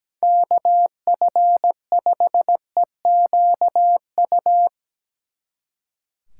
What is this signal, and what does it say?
Morse code, heard as a keyed tone, about 19 wpm, 700 Hz. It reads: KF5EQU
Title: Speed Racing - USA 2x3 Call Signs 17wpm